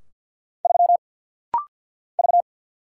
Morse code, heard as a keyed tone, about 50 wpm, 700 Hz. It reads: 3 V
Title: Single Letter-Number - Rapid-Fire 50wpm